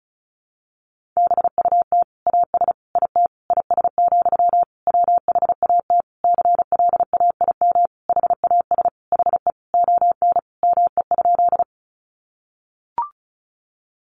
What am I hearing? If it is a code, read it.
Morse code, heard as a keyed tone, about 34 wpm, 700 Hz. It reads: BUT AS IT IS, WHAT CLAIM HAS HE ON ME?
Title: Sentences from Top 1500 Words 35wpm